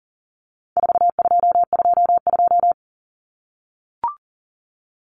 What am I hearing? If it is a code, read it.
Morse code, heard as a keyed tone, about 40 wpm, 700 Hz. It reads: 4222
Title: Common Cut Numbers - 4-Digit 40wpm